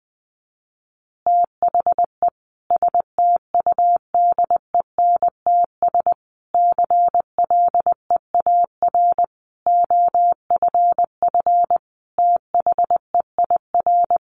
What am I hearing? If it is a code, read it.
Morse code, heard as a keyed tone, about 19 wpm, 700 Hz. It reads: THE STUDENTS CLEAR OFF THEIR
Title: Additional Easy Phrases 20wpm